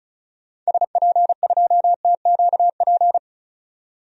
Morse code, heard as a keyed tone, about 34 wpm, 700 Hz. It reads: SP2TQP
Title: DXCC-100 Call Signs - 6 Characters - 35wpm